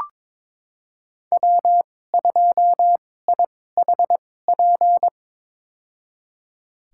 Morse code, heard as a keyed tone, about 22 wpm, spaced as for 16 wpm, 700 Hz. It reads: W2IHP
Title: USA Call Signs - 2x Character Spacing - 22wpm